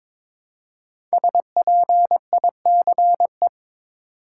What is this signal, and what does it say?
Morse code, heard as a keyed tone, about 22 wpm, 700 Hz. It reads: SPICE
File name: 5 Letter Words - First 3 Letters Given - 22wpm